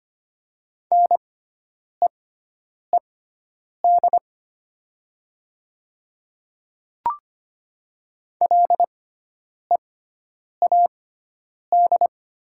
Morse code, heard as a keyed tone, about 25 wpm, spaced as for 9 wpm, 700 Hz. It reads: NEED LEAD
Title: 6x Character Spacing - Top 500 Words - 25wpm